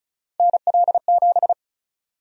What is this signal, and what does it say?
Morse code, heard as a keyed tone, about 36 wpm, 700 Hz. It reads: NL7
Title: USA Call Sign Prefixes - 35wpm